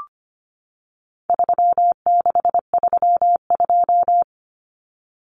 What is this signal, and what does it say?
Morse code, heard as a keyed tone, about 25 wpm, 700 Hz. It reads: 3632
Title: Common Cut Numbers - 4-Digit 25wpm